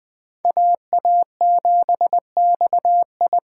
Morse code, heard as a keyed tone, about 21 wpm, 700 Hz. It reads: AA7XI